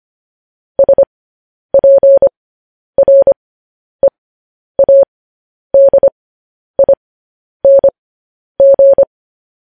Morse code, heard as a keyed tone, about 25 wpm, spaced as for 10 wpm, 550 Hz. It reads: SPREADING
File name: podcast / News headlines-1x-25x10